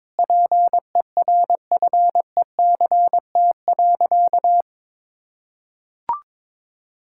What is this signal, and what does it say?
Morse code, heard as a keyed tone, about 22 wpm, 700 Hz. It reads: PERFECT.
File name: Sentences with 9-Word Phrases - 22wpm